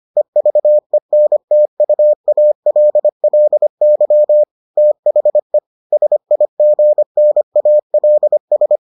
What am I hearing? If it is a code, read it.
Morse code, heard as a keyed tone, about 25 wpm, 600 Hz. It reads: EVENTUALLY THE SIGNALS